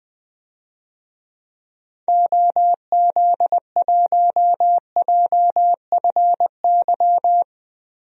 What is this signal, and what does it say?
Morse code, heard as a keyed tone, about 20 wpm, 700 Hz. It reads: OZ1JFY
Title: DXCC-100 Call Signs - 6 Characters - 20wpm